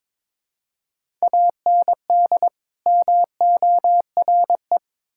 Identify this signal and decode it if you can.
Morse code, heard as a keyed tone, about 22 wpm, 700 Hz. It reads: AND MORE
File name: Build A Phrase - 2-Word Phrases - 22wpm